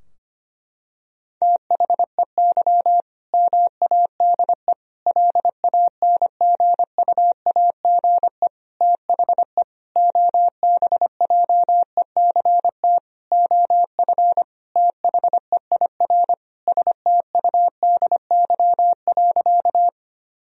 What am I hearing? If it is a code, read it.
Morse code, heard as a keyed tone, about 25 wpm, 700 Hz. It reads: THEY MADE LANGUAGE THE OBJECT OF THEIR STUDY.
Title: Sentences from Top 1000 Words - Review Part 3 - 25wpm